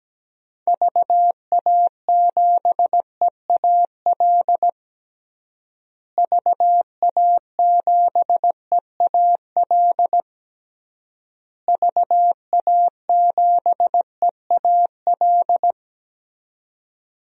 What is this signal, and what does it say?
Morse code, heard as a keyed tone, about 17 wpm, 700 Hz. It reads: VA7EAL VA7EAL VA7EAL
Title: DXCC-100 Call Signs - 6 Characters - Repeated 3x - 17wpm